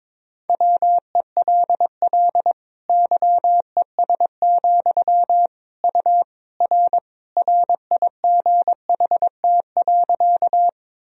Morse code, heard as a keyed tone, about 22 wpm, 700 Hz. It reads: WELL YES, U R RIGHT.
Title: Sentences from Top 500 Words and Abbreviations 22wpm